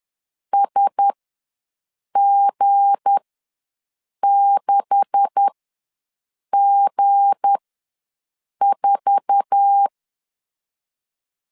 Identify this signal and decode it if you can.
Morse code, heard as a keyed tone, about 10 wpm, spaced as for 6 wpm, 800 Hz. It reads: SG6G4
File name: LEZIONI / 9 286 ws 10wpm 4032